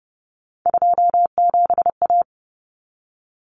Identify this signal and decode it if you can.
Morse code, heard as a keyed tone, about 30 wpm, 700 Hz. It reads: 27A
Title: Common Cut Numbers - 3-Digit 30wpm